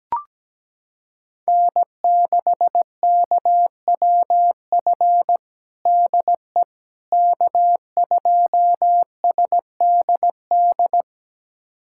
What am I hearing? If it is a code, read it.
Morse code, heard as a keyed tone, about 17 wpm, 700 Hz. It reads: N6KWF DE K2SDD
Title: USA Callsign DE Callsign - 5 Characters - 17wpm